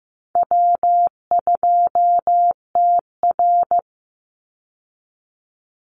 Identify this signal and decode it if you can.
Morse code, heard as a keyed tone, about 15 wpm, 700 Hz. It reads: W2TR
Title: Easy Call Signs Repeated 2x - 15wpm